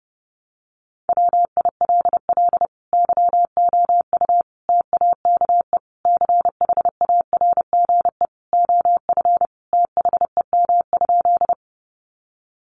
Morse code, heard as a keyed tone, about 30 wpm, 700 Hz. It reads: WILL YOU TAKE CHARGE OF THEM?
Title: Sentences from Top 900 Words 30wpm